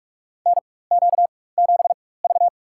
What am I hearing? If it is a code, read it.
Morse code, heard as a keyed tone, about 46 wpm, spaced as for 21 wpm, 700 Hz. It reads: NQ7V
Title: USA Call Signs - 4x Character Spacing - 45wpm